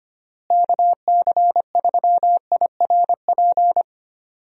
Morse code, heard as a keyed tone, about 25 wpm, 700 Hz. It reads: KC3IRP
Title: USA Call Signs - 6 Characters - Repeated 3x - 25wpm